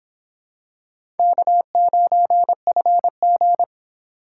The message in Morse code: K9FG

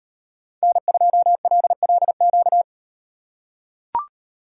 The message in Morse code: N2LLQ

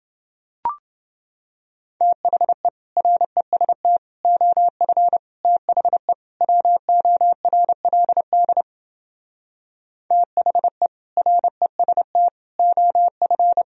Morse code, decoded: THE REST OF THE WORLD THE REST OF

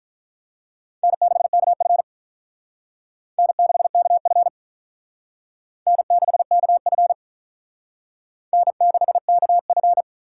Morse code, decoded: N6KF N6KF N6KF N6KF

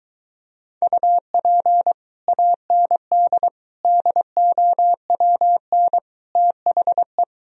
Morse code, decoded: UP AND DOWN THE

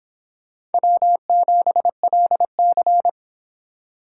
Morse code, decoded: W7LC

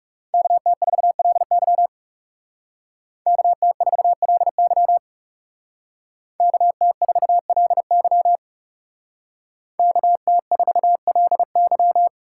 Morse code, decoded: KT4LY KT4LY KT4LY KT4LY

